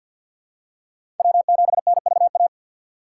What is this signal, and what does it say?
Morse code, heard as a keyed tone, about 52 wpm, 700 Hz. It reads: W7NVA
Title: USA Call Signs - 5 Characters - 3x Character Spacing - 50wpm